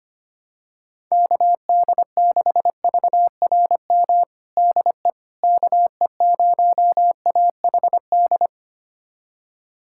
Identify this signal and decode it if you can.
Morse code, heard as a keyed tone, about 25 wpm, 700 Hz. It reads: KD6VRM DE KE0AHD
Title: USA Callsign DE Callsign 25wpm